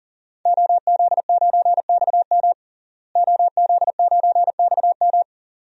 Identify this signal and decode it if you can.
Morse code, heard as a keyed tone, about 37 wpm, 700 Hz. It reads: OZ9XM OZ9XM
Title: DXCC-100 Call Signs - 5 Characters - Repeated 2x - 40wpm